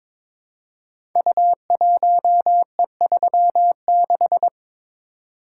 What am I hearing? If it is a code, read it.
Morse code, heard as a keyed tone, about 22 wpm, 700 Hz. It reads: U1E36
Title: Code Groups - 5 Letter:Numbers - 22wpm